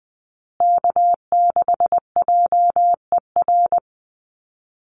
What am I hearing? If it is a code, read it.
Morse code, heard as a keyed tone, about 20 wpm, 700 Hz. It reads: K6JER
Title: USA Call Signs Repeated 3x 20wpm